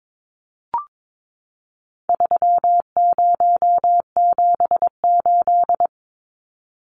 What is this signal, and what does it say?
Morse code, heard as a keyed tone, about 22 wpm, 700 Hz. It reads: 3078
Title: Numbers - 4-Digit 22wpm